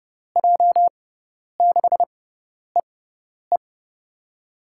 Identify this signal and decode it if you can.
Morse code, heard as a keyed tone, about 29 wpm, spaced as for 10 wpm, 700 Hz. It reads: J6EE
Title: USA Call Signs - 5 Characters - 6x Character Spacing - 30wpm